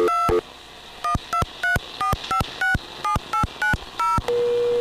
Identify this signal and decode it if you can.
DTMF, heard as telephone touch tones, tones of about 130 ms. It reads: 2 1234567890